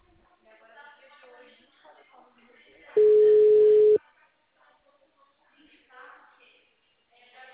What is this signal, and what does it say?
Ringback tone (Europe, 425 Hz), with speech.